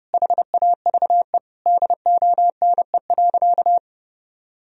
Morse code, heard as a keyed tone, about 30 wpm, 700 Hz. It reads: HAVE DONE.
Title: Sentences from Top 1000 Words - Review Part 3 - 30wpm